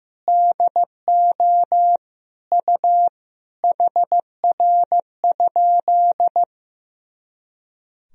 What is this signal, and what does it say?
Morse code, heard as a keyed tone, about 15 wpm, 700 Hz. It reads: DO U HR?
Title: Sentences from Top 300 Words and Abbreviations 15wpm